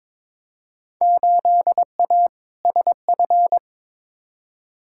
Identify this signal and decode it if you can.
Morse code, heard as a keyed tone, about 22 wpm, 700 Hz. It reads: 8A SF
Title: Field Day Exchanges Encore - 22wpm